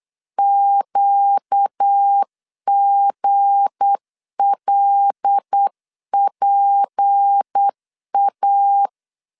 Morse code, heard as a keyed tone, about 8 wpm, 800 Hz. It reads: QGLPA
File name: LEZIONI / 7 GF9Z / 7 G 8wpm 3118